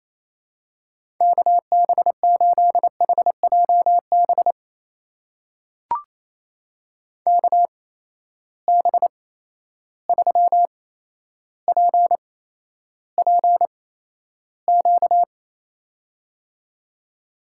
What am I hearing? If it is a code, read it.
Morse code, heard as a keyed tone, about 28 wpm, spaced as for 12 wpm, 700 Hz. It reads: KB8HJB KB3PPQ
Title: USA Call Signs - 6 Characters - 8x Character Spacing - 28wpm